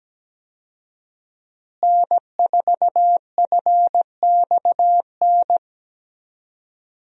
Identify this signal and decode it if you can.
Morse code, heard as a keyed tone, about 17 wpm, 700 Hz. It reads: N4FXN
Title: USA Call Signs - 5 Characters - Repeated 2x - 17wpm